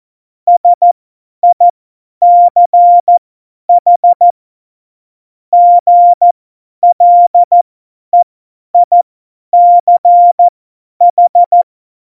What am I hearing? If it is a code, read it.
Morse code, heard as a keyed tone, about 14 wpm, spaced as for 10 wpm, 700 Hz. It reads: SICH GLEICH
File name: Morsekurs / Witze / Witz 08. Oktober 2019, 50 F70